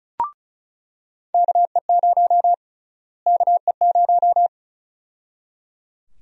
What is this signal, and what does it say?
Morse code, heard as a keyed tone, about 34 wpm, 700 Hz. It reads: KE0 KE0
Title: USA Call Sign Prefixes - Repeated 2x - 35wpm